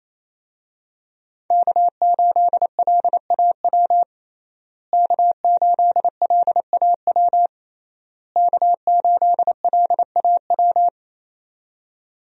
Morse code, heard as a keyed tone, about 28 wpm, 700 Hz. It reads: K8LAW K8LAW K8LAW
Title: USA Call Signs - 5 Characters - Repeated 3x - 28wpm